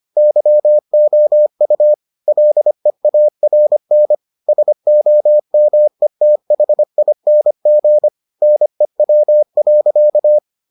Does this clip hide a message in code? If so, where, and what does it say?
Yes: Morse code from 0.2 to 10.4 s, heard as a keyed tone, about 25 wpm, 600 Hz. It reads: YOU LEARN SOMETHING NEW.